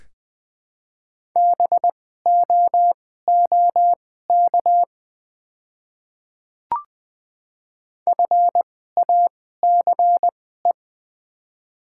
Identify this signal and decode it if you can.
Morse code, heard as a keyed tone, about 20 wpm, spaced as for 15 wpm, 700 Hz. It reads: BOOK FACE
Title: 2x Character Spacing - Top 300 Words - 20wpm